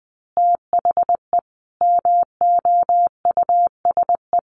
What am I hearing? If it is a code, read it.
Morse code, heard as a keyed tone, about 20 wpm, 700 Hz. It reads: THE MOUSE
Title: Easy Phrases 20wpm